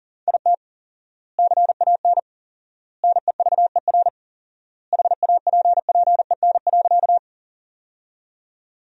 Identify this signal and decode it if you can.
Morse code, heard as a keyed tone, about 40 wpm, 700 Hz. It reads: IT CAN NEVER HAPPEN.
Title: 4x Word-Spacing - Sentences from Top 400 Words - 40wpm